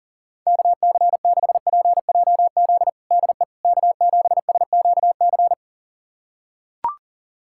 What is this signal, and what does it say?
Morse code, heard as a keyed tone, about 37 wpm, 700 Hz. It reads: KC6PJZ DE K7SQC
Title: USA Callsign DE Callsign 40wpm